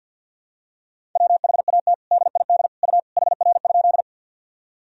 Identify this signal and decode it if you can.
Morse code, heard as a keyed tone, about 52 wpm, 700 Hz. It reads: WHAT DID U HR?
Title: Sentences from Top 300 Words and Abbreviations 50wpm